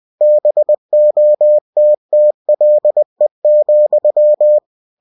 Morse code, heard as a keyed tone, about 20 wpm, 600 Hz. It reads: BOTTLE,
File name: cw-books / winnie-the-pooh / winnie-20wpm-page-0092